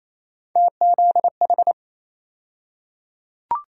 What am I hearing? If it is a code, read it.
Morse code, heard as a keyed tone, about 28 wpm, 700 Hz. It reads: TZH
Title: USA Call Sign Postfixes - 3 Letters 28wpm